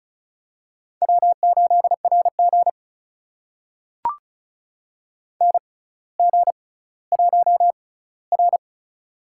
Morse code, heard as a keyed tone, about 36 wpm, spaced as for 18 wpm, 700 Hz. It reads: W8RG NG1R
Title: USA Call Signs - 4 Characters - 6x Character Spacing - 35wpm